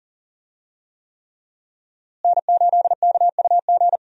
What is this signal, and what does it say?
Morse code, heard as a keyed tone, about 40 wpm, 700 Hz. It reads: N8KUG
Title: USA Call Signs - 5 Characters - Repeated 3x - 40wpm